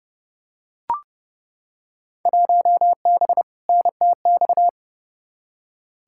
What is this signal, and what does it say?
Morse code, heard as a keyed tone, about 30 wpm, 700 Hz. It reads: 1B NTX
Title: Field Day Exchanges - 30wpm